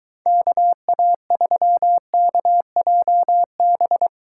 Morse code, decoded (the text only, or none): KA3KJB